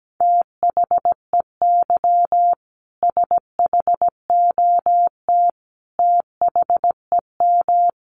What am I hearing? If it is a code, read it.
Morse code, heard as a keyed tone, about 17 wpm, 700 Hz. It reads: THEY SHOT THEM
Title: Sentences from Top 1100 Words 17wpm